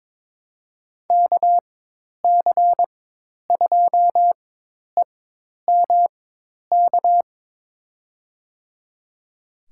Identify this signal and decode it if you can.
Morse code, heard as a keyed tone, about 23 wpm, spaced as for 10 wpm, 700 Hz. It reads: KC2EMK
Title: USA Call Signs - 6 Characters - 4x Character Spacing - 22wpm